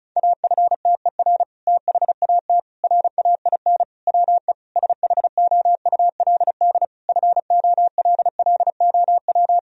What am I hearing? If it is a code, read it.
Morse code, heard as a keyed tone, about 34 wpm, 700 Hz. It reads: AFTER THAT RAIN WE SHOULD FOLLOW
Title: Sentences from Top 1000 Words - Review Part 1 - 35wpm